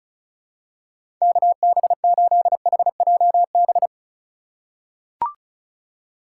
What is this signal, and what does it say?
Morse code, heard as a keyed tone, about 36 wpm, 700 Hz. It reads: KB8HJB
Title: USA Call Signs - 6 Characters - 2x Character Spacing - 35wpm